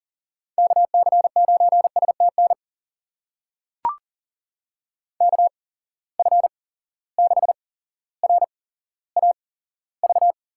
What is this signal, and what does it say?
Morse code, heard as a keyed tone, about 40 wpm, spaced as for 17 wpm, 700 Hz. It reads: KC9STN KF6RAV
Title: USA Call Signs - 6 Characters - 8x Character Spacing - 40wpm